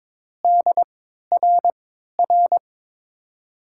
Morse code, heard as a keyed tone, about 22 wpm, spaced as for 13 wpm, 700 Hz. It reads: DRR
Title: USA Call Signs - 5 Characters - 3x Character Spacing - 22wpm